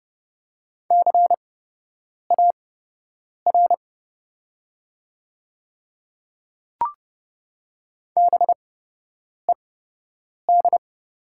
Morse code, heard as a keyed tone, about 30 wpm, spaced as for 8 wpm, 700 Hz. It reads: CAR BED
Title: 8x Character Spacing - Top 500 Words - 30wpm